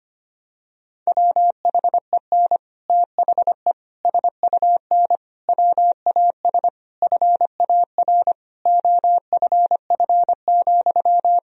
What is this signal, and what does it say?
Morse code, heard as a keyed tone, about 25 wpm, 700 Hz. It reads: WHEN THE SUN WAS FAR OFF,
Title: Sentences from Top 1000 Words - Review Part 3 - 25wpm